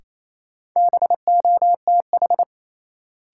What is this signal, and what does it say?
Morse code, heard as a keyed tone, about 28 wpm, 700 Hz. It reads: BOTH
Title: Sets of 4 Words - New Word Drill - 28wpm